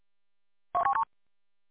There is no code.